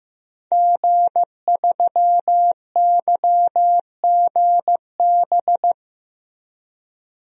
Morse code, heard as a keyed tone, about 15 wpm, 700 Hz. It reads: G3YGB
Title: DXCC-100 Call Signs - 5 Characters - 15wpm